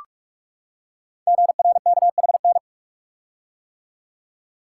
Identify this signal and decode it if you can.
Morse code, heard as a keyed tone, about 44 wpm, 700 Hz. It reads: GRKHN
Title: Code Groups - 5 Letter:Numbers - 45wpm